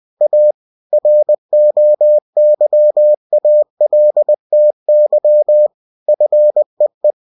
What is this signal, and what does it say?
Morse code, heard as a keyed tone, about 20 wpm, 600 Hz. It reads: A ROYALTY FEE
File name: cw-books / winnie-the-pooh / winnie-20wpm-page-0117